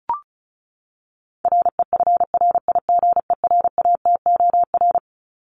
Morse code, heard as a keyed tone, about 34 wpm, 700 Hz. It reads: REFRIGERATOR
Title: Top 1501-1750 Words 35wpm